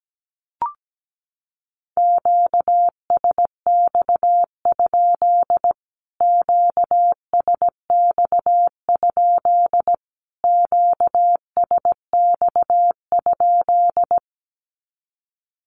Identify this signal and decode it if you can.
Morse code, heard as a keyed tone, about 17 wpm, 700 Hz. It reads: QSX? QSX? QSX?
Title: Q-Codes Repeated 3x 17wpm